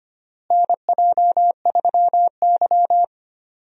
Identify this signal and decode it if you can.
Morse code, heard as a keyed tone, about 25 wpm, 700 Hz. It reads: NJ3Y
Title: USA Call Signs - 4 Characters - Repeated 3x - 25wpm